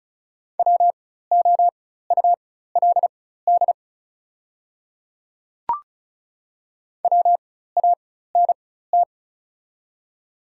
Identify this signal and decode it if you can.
Morse code, heard as a keyed tone, about 34 wpm, spaced as for 16 wpm, 700 Hz. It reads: WOULD WANT
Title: 4x Character Spacing - Top 200 Words - 35wpm